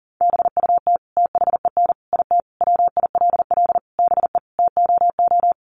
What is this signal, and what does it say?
Morse code, heard as a keyed tone, about 40 wpm, 700 Hz. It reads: BUT THEN IT WILL BE TOO
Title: Sentences from Top 300 Words 40wpm